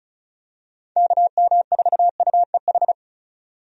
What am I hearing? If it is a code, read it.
Morse code, heard as a keyed tone, about 34 wpm, 700 Hz. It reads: KM4UEH
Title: USA Call Signs - 6 Characters - 35wpm